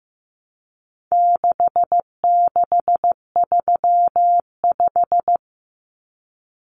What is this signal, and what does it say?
Morse code, heard as a keyed tone, about 15 wpm, 700 Hz. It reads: 6635
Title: Numbers - 4-Digit 15wpm